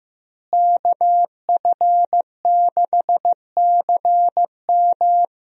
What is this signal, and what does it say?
Morse code, heard as a keyed tone, about 16 wpm, 700 Hz. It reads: KF6CM
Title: USA Call Signs - 5 Characters - Repeated 2x - 15wpm